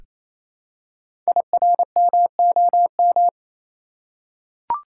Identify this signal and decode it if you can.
Morse code, heard as a keyed tone, about 29 wpm, 700 Hz. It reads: IRMOM
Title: Code Groups - 5 Letters - 28wpm